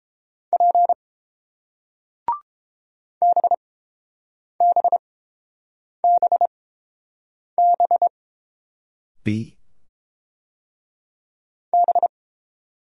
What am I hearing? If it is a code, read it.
Morse code, heard as a keyed tone, about 30 wpm, spaced as for 7 wpm, 700 Hz. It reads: P BBBB B